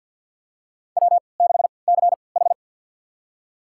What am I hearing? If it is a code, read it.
Morse code, heard as a keyed tone, about 53 wpm, spaced as for 29 wpm, 700 Hz. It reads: W6CH
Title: USA Call Signs - 4 Characters - 3x Character Spacing - 50wpm